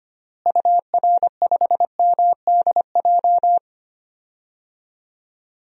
Morse code, heard as a keyed tone, about 25 wpm, 700 Hz. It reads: UR5MDJ
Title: DXCC-100 Call Signs - 6 Characters - Repeated 3x - 25wpm